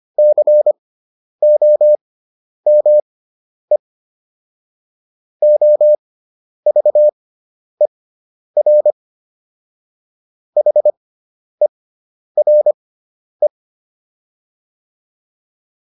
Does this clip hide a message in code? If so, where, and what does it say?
Yes: Morse code from 0.2 to 13.5 s, heard as a keyed tone, about 25 wpm, spaced as for 10 wpm, 600 Hz. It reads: COME OVER HERE